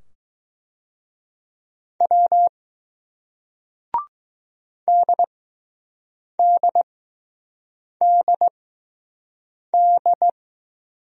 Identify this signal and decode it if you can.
Morse code, heard as a keyed tone, about 20 wpm, spaced as for 6 wpm, 700 Hz. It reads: W DDDD